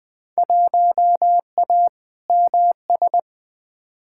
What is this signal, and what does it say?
Morse code, heard as a keyed tone, about 20 wpm, 700 Hz. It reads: 1A MS
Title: Field Day Exchanges - 20wpm